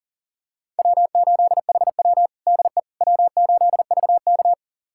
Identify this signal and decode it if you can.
Morse code, heard as a keyed tone, about 40 wpm, 700 Hz. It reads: W8HW DE W8VK
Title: USA Callsign DE Callsign - 4 Characters - 40wpm